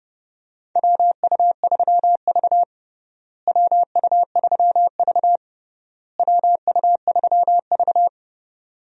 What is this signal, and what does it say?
Morse code, heard as a keyed tone, about 32 wpm, 700 Hz. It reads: WU3V WU3V WU3V
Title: USA Call Signs - 4 Characters - Repeated 3x - 30wpm